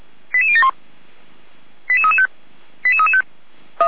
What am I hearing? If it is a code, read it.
Five-tone selective call (ZVEI), heard as quick runs of five tones, tones of about 70 ms. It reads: 88071 88335 88335
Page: SDR/ZVEI